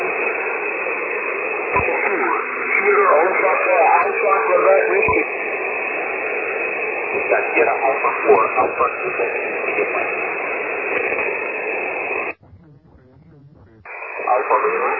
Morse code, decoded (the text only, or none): none